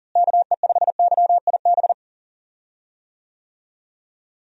Morse code, heard as a keyed tone, about 40 wpm, 700 Hz. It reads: KE5YIB
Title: USA Call Signs - 6 Characters - Repeated 3x - 40wpm